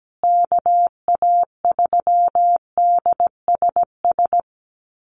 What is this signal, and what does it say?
Morse code, heard as a keyed tone, about 17 wpm, 700 Hz. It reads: KA3DSS
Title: USA Call Signs Repeated 3x 17wpm